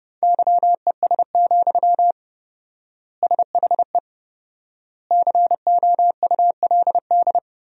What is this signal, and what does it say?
Morse code, heard as a keyed tone, about 30 wpm, 700 Hz. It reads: YES, SHE COULD
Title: 4x Word-Spacing - Sentences from Top 500 Words - 30wpm